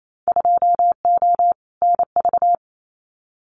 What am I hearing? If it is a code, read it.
Morse code, heard as a keyed tone, about 28 wpm, 700 Hz. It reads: 2O NV
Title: Winter Field Day Exchanges 28wpm